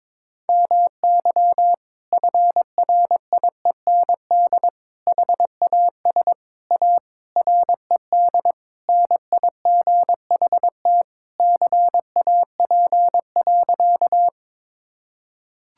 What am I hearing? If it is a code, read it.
Morse code, heard as a keyed tone, about 22 wpm, 700 Hz. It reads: MY FRIEND HAS A RED NIGHT CAP.
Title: Sentences from Top 1600 Words 22wpm